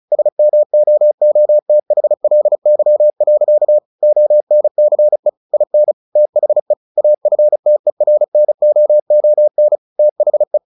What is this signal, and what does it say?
Morse code, heard as a keyed tone, about 34 wpm, 600 Hz. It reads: SMOOTHLY. ONCE IN THE AFTERNOON THE
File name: the old man and the sea / the old man and the sea 1 35wpm